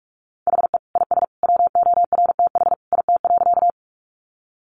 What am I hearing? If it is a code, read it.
Morse code, heard as a keyed tone, about 44 wpm, 700 Hz. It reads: HE IS WORTH IT.
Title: Sentences from Top 800 Words 45wpm